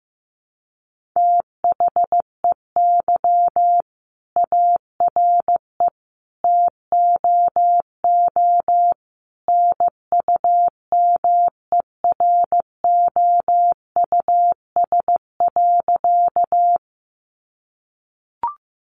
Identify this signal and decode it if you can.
Morse code, heard as a keyed tone, about 15 wpm, 700 Hz. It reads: THEY ARE TOO NUMEROUS.
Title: Sentences from Top 1400 Words 15wpm